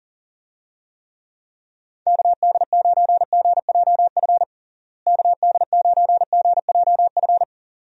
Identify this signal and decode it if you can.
Morse code, heard as a keyed tone, about 40 wpm, 700 Hz. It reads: KD9GJF KD9GJF
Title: USA Call Signs - 6 Characters - Repeated 2x - 40wpm